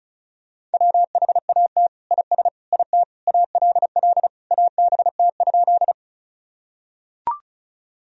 Morse code, heard as a keyed tone, about 36 wpm, 700 Hz. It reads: WHAT IS IT ALL ABT?
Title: Sentences from Top 100 Words and Abbreviations 35wpm